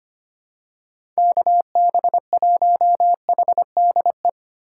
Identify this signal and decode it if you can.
Morse code, heard as a keyed tone, about 25 wpm, 700 Hz. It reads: KB1HDE
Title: USA Call Signs - 6 Characters - 8x Character Spacing - 25wpm